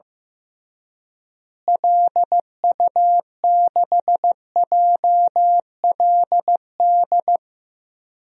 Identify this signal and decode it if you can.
Morse code, heard as a keyed tone, about 15 wpm, 700 Hz. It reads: LU6JLD
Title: DXCC-100 Call Signs - 6 Characters - Repeated 3x - 15wpm